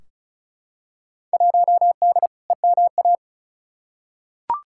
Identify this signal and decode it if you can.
Morse code, heard as a keyed tone, about 36 wpm, 700 Hz. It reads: 1D EMA
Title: Field Day Exchanges - 35wpm